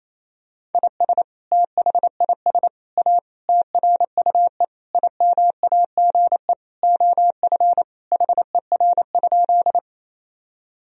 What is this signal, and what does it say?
Morse code, heard as a keyed tone, about 28 wpm, 700 Hz. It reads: IS THIS A TRUE IMAGE OF HER?
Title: Sentences from Top 1000 Words - Review Part 4 - 28wpm